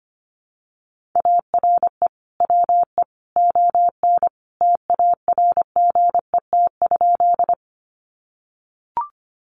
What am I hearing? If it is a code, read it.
Morse code, heard as a keyed tone, about 25 wpm, 700 Hz. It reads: ARE WE ON TARGET?
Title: Sentences from Top 1200 Words 25wpm